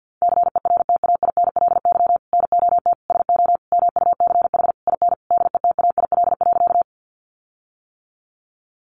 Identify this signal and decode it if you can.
Morse code, heard as a keyed tone, about 50 wpm, 700 Hz. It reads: CERTAINLY NOT SO MUCH IN DETAIL.
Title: Sentences from Top 1500 Words 50wpm